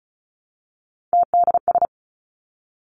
Morse code, heard as a keyed tone, about 36 wpm, 700 Hz. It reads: TDS
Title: Three Challenging Letter-Number Combinations 35wpm